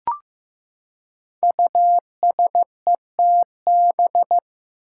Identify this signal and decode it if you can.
Morse code, heard as a keyed tone, about 15 wpm, 700 Hz. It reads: USETB
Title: Five Challenging Letter-Number Combinations 15wpm